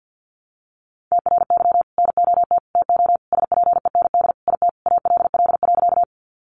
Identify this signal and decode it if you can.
Morse code, heard as a keyed tone, about 50 wpm, 700 Hz. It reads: TRY NOT TO SPEND IT ALL.
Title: Sentences from Top 800 Words 50wpm